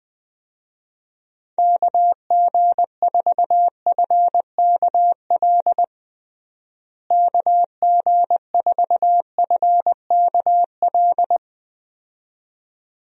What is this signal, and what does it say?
Morse code, heard as a keyed tone, about 20 wpm, 700 Hz. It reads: KG4FKL KG4FKL
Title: USA Call Signs - 6 Characters - Repeated 2x - 20wpm